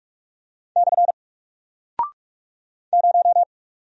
Morse code, heard as a keyed tone, about 44 wpm, 700 Hz. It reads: / 0
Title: Single Character (Letter:Number:Punctuation:Prosign) Alt 45wpm